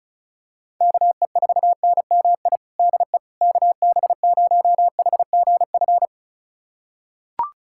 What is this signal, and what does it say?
Morse code, heard as a keyed tone, about 34 wpm, 700 Hz. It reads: KE4NMI DE KB0HGF
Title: USA Callsign DE Callsign - 6 Characters - 35wpm